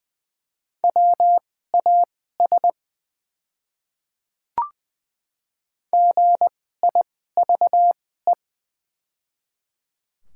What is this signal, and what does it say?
Morse code, heard as a keyed tone, about 20 wpm, spaced as for 14 wpm, 700 Hz. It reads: WAS GIVE
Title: 2x Character Spacing - Top 200 Words - 20wpm